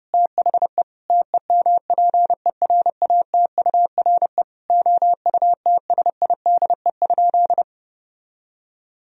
Morse code, heard as a keyed tone, about 30 wpm, 700 Hz. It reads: THE TEMPERATURE OUTSIDE?
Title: Sentences from Top 1000 Words - Review Part 4 - 30wpm